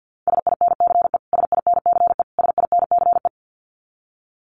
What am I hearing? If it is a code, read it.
Morse code, heard as a keyed tone, about 50 wpm, 700 Hz. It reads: SINCE SINCE SINCE
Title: Top 300 Words Repeated 50wpm